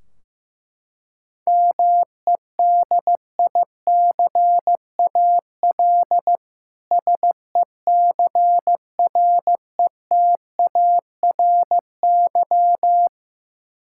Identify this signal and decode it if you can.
Morse code, heard as a keyed tone, about 15 wpm, 700 Hz. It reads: MEDICAL SECRETARY
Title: K1USN SST QSO Practice 15wpm